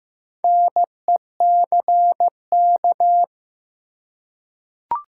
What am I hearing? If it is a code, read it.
Morse code, heard as a keyed tone, about 15 wpm, 700 Hz. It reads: NECK
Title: Four Letter Words - 15wpm